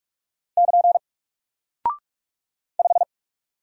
Morse code, heard as a keyed tone, about 44 wpm, 700 Hz. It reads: <KN> 5
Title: Single Character (Letter:Number:Punctuation:Prosign) Alt 45wpm